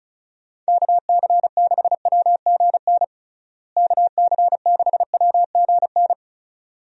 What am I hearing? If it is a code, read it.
Morse code, heard as a keyed tone, about 36 wpm, 700 Hz. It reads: KC6WGN KC6WGN
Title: USA Call Signs - 6 Characters - Repeated 2x - 35wpm